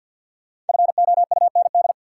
Morse code, heard as a keyed tone, about 52 wpm, 700 Hz. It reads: FOUND
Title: Sets of 5 Words Encore - New Word Drill - 50wpm